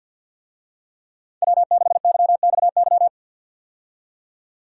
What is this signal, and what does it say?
Morse code, heard as a keyed tone, about 51 wpm, 700 Hz. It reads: W6YXY
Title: USA Call Signs - 5 Characters - 50wpm